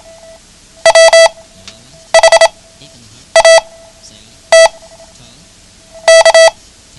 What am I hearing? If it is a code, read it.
Morse code, heard as a keyed tone, about 28 wpm, spaced as for 8 wpm, 660 Hz. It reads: WHAT K
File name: ham radio / CodeQuick / Code Quick 09